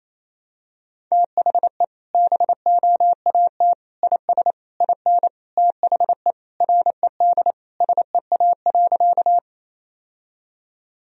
Morse code, heard as a keyed tone, about 28 wpm, 700 Hz. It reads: THE BOAT IS IN THE RED SEA.
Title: Sentences from Top 1000 Words - Review Part 2 - 28wpm